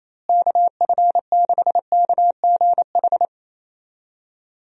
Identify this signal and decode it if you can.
Morse code, heard as a keyed tone, about 28 wpm, 700 Hz. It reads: KF6KGH
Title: USA Call Signs - 6 Characters - Repeated 3x - 28wpm